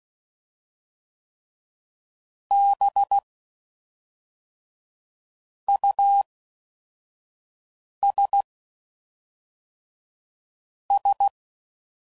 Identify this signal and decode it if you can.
Morse code, heard as a keyed tone, about 16 wpm, 800 Hz. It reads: B U S S
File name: Suffixes / Special INGs / Fast INGs